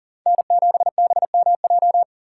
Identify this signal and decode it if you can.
Morse code, heard as a keyed tone, about 40 wpm, 700 Hz. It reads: N7BMJ
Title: USA Call Signs - 5 Characters - 6x Character Spacing - 40wpm